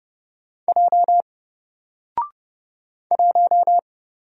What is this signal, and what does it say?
Morse code, heard as a keyed tone, about 30 wpm, 700 Hz. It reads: J 1
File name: Single-Letter-Number-30wpm